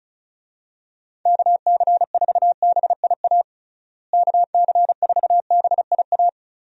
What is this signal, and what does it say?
Morse code, heard as a keyed tone, about 34 wpm, 700 Hz. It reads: KC4BIA KC4BIA
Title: USA Call Signs - 6 Characters - Repeated 2x - 35wpm